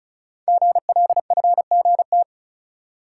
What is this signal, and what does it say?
Morse code, heard as a keyed tone, about 36 wpm, 700 Hz. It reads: GLFGT
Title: Code Groups - 5 Letters - 35wpm